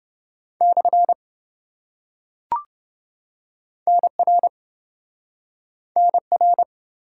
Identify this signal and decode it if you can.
Morse code, heard as a keyed tone, about 27 wpm, 700 Hz. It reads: / NR NR